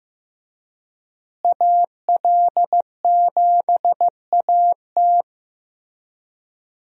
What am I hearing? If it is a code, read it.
Morse code, heard as a keyed tone, about 15 wpm, 700 Hz. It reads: AL7AT
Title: USA Call Signs - 5 Characters - Repeated 3x - 15wpm